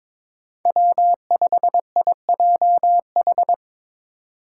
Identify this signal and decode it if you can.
Morse code, heard as a keyed tone, about 22 wpm, 700 Hz. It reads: W5IJH
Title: K1USN SST QSO Practice - 22wpm